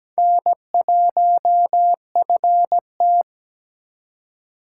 Morse code, heard as a keyed tone, about 20 wpm, 700 Hz. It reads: N1FT
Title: Speed Racing - USA 1x2 Call Signs 17wpm